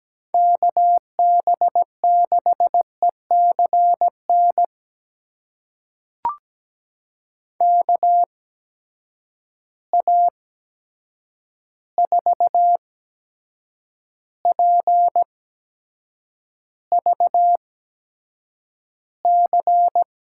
Morse code, heard as a keyed tone, about 17 wpm, spaced as for 7 wpm, 700 Hz. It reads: KB6ECN KA4PVC
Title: USA Call Signs - 6 Characters - 8x Character Spacing - 17wpm